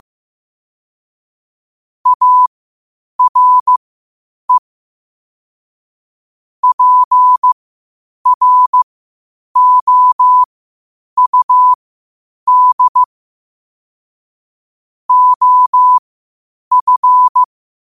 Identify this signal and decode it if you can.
Morse code, heard as a keyed tone, about 15 wpm, spaced as for 8 wpm, 1000 Hz. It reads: ARE PROUD OF
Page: 8 wpm morse code quotes